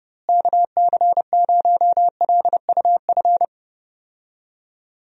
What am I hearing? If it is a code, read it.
Morse code, heard as a keyed tone, about 30 wpm, 700 Hz. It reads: KC0LUF
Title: USA Call Signs - 6 Characters - Repeated 2x - 30wpm